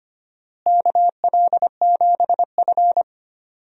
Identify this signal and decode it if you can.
Morse code, heard as a keyed tone, about 25 wpm, 700 Hz. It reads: KL7F
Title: DXCC-100 Call Signs - 4 Characters - Repeated 3x - 25wpm